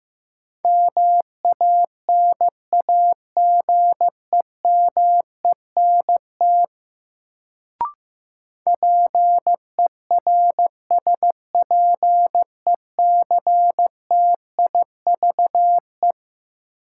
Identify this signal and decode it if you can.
Morse code, heard as a keyed tone, about 15 wpm, 700 Hz. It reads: MANAGEMENT PERSPECTIVE
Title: Sentences from Top 1200 Words - New Word Drill Alt - 15wpm